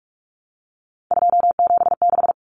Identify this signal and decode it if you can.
Morse code, heard as a keyed tone, about 44 wpm, 700 Hz. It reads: 276
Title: Common Cut Numbers - 3-Digit 45wpm